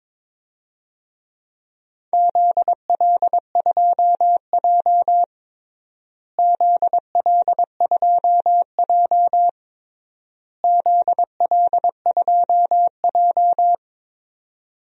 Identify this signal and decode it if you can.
Morse code, heard as a keyed tone, about 23 wpm, 700 Hz. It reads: ZL2J ZL2J ZL2J
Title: DXCC-100 Call Signs - 4 Characters - Repeated 3x - 22wpm